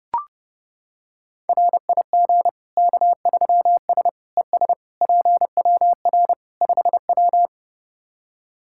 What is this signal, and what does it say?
Morse code, heard as a keyed tone, about 30 wpm, 700 Hz. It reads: RIG K3S ES PWR 5W
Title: K1USN SST QSO Practice 30wpm